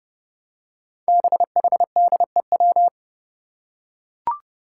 Morse code, heard as a keyed tone, about 30 wpm, 700 Hz. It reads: BHDEW
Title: Five Challenging Letter-Number Combinations 30wpm